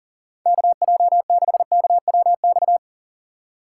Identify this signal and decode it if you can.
Morse code, heard as a keyed tone, about 39 wpm, 700 Hz. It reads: KJ6KWX
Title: USA Call Signs - 6 Characters - 2x Character Spacing - 40wpm